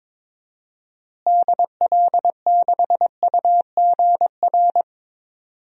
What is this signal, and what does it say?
Morse code, heard as a keyed tone, about 22 wpm, 700 Hz. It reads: DL6UGR
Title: DXCC-100 Call Signs - 6 Characters - Repeated 3x - 22wpm